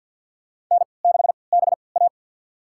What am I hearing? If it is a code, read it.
Morse code, heard as a keyed tone, about 49 wpm, spaced as for 28 wpm, 700 Hz. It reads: N6BA
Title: USA Call Signs - 4 Characters - 3x Character Spacing - 50wpm